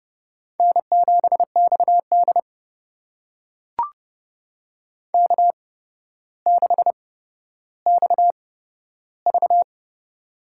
Morse code, heard as a keyed tone, about 32 wpm, spaced as for 13 wpm, 700 Hz. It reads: N7XD K6XV
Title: USA Call Signs - 4 Characters - 8x Character Spacing - 30wpm